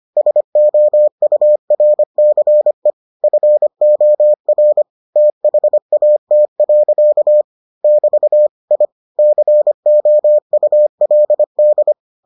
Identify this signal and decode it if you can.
Morse code, heard as a keyed tone, about 25 wpm, 600 Hz. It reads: SOURCE FOR THAT. <BT> I COULD